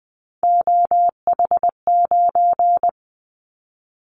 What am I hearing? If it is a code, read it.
Morse code, heard as a keyed tone, about 20 wpm, 700 Hz. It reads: OH9
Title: Callsign Prefixes for Top 50 DXCC - 20wpm